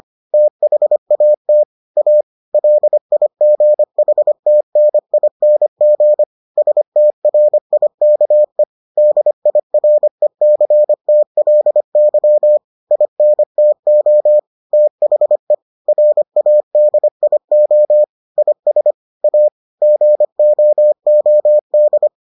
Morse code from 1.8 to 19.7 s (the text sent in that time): A LIGHTNING STRIKE DIRECTLY INTO THE RADIO IS A